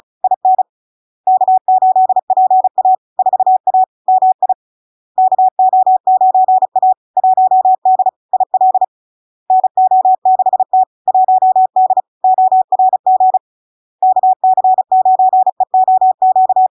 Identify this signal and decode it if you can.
Morse code, heard as a keyed tone, about 35 wpm, 750 Hz. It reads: IN K8PA 4A MI KO9A 1D IL NO6T 1D ORG KC9EOQ
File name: LARK-FD-2022-Log-in-CW-35wpm-750-Hz